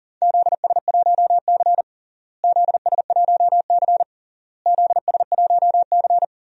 Morse code, heard as a keyed tone, about 40 wpm, 700 Hz. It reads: ZS1C ZS1C ZS1C
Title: DXCC-100 Call Signs - 4 Characters - Repeated 3x - 40wpm